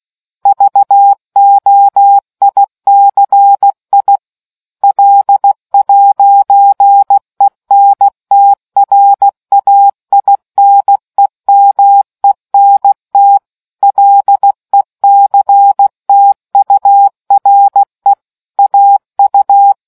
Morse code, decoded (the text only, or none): VOICI L'ENTRAINEMENT LECTURE AU